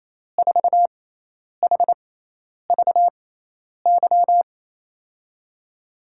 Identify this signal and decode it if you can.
Morse code, heard as a keyed tone, about 27 wpm, spaced as for 10 wpm, 700 Hz. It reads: 4HVY
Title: USA Call Signs - 6x Character Spacing - 28wpm